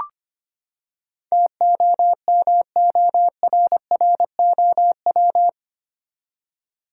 Morse code, heard as a keyed tone, about 25 wpm, 700 Hz. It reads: TOMORROW
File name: Eight Letter Words - 25wpm